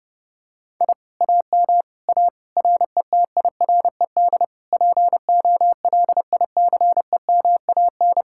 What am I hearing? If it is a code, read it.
Morse code, heard as a keyed tone, about 29 wpm, 700 Hz. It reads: I AM A RETIRED POLICEMAN